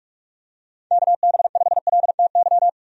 Morse code, heard as a keyed tone, about 46 wpm, 700 Hz. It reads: KB5LTY
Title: USA Call Signs - 6 Characters - 45wpm